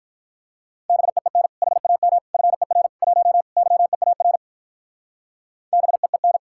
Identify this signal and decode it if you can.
Morse code, heard as a keyed tone, about 44 wpm, 700 Hz. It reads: BEEN HAM FER 1 YEAR BEEN